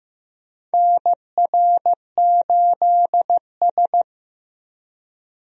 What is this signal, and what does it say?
Morse code, heard as a keyed tone, about 16 wpm, 700 Hz. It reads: NR8S
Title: USA Call Signs - 4 Characters - Repeated 2x - 15wpm